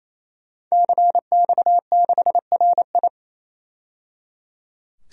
Morse code, heard as a keyed tone, about 28 wpm, 700 Hz. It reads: CX6RI
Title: DXCC-100 Call Signs - 5 Characters - 28wpm